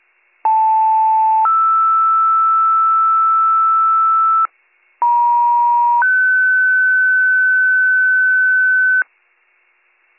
Two-tone page, heard as a one-second tone, then a three-second tone, about 880 then 1320 Hz, about 950 then 1500 Hz. It is heard from 0.4 to 9.0 s.